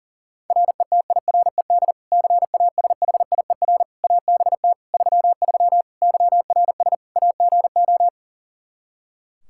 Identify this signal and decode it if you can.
Morse code, heard as a keyed tone, about 40 wpm, 700 Hz. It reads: RETIRED CASHIER ABT 33 YRS AGO
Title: Practice QSO - Exchange 3 - More Info - 40wpm